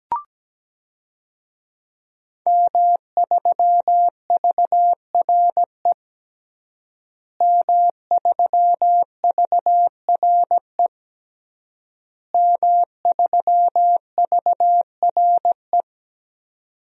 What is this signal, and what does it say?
Morse code, heard as a keyed tone, about 17 wpm, 700 Hz. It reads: M3VRE M3VRE M3VRE
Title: DXCC-100 Call Signs - 5 Characters - Repeated 3x - 17wpm